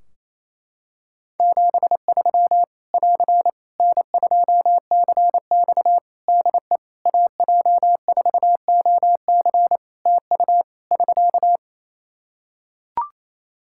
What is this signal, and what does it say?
Morse code, heard as a keyed tone, about 28 wpm, 700 Hz. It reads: 73 <AR> N2CX DE AJ4OC TU <SK>
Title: Practice QSO - Exchange 4 - Ending - 28wpm